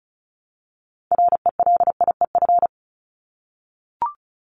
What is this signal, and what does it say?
Morse code, heard as a keyed tone, about 34 wpm, 700 Hz. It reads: RELIEF
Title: Top 1501-1750 Words 35wpm